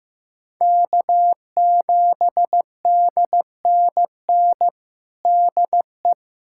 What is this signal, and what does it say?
Morse code, heard as a keyed tone, about 16 wpm, 700 Hz. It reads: K7DNN DE
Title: USA Callsign DE Callsign - 5 Characters - 15wpm